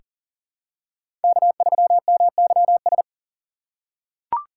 Morse code, heard as a keyed tone, about 40 wpm, 700 Hz. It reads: K3MYS
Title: USA Call Signs - 5 Characters - 4x Character Spacing - 40wpm